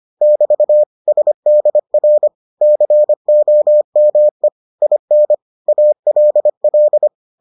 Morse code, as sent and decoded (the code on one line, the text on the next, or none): -...- / ... -.. .-. / -.-. --- -- . / .. -. / .- .-.. .-..
<BT> SDR COME IN ALL